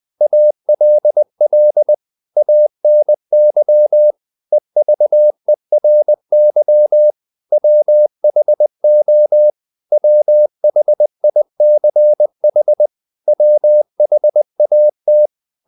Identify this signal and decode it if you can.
Morse code, heard as a keyed tone, about 20 wpm, 600 Hz. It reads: ALL ANY EVERY WHO WHICH WHAT
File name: CW / Practice / 500MostCommonEnglishWords20